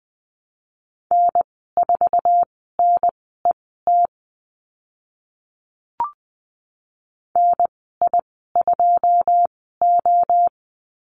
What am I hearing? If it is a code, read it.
Morse code, heard as a keyed tone, about 20 wpm, spaced as for 15 wpm, 700 Hz. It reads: N4NET NI2O
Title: 088 Lesson 15g - Number 5 - Call Signs 20wpm